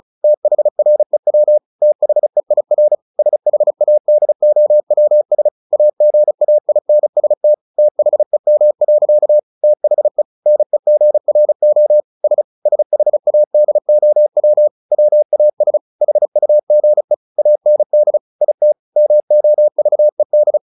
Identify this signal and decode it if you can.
Morse code, heard as a keyed tone, about 34 wpm, 600 Hz. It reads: THREW THEIR SHADOWS AGAINST THEM. THE NEGRO S SHADOW WAS HUGE AND IT MOVED
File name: the old man and the sea 2 35wpm